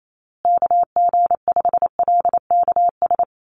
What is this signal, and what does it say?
Morse code, heard as a keyed tone, about 28 wpm, 700 Hz. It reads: KG5LKS
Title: USA Call Signs Repeated 3x 28wpm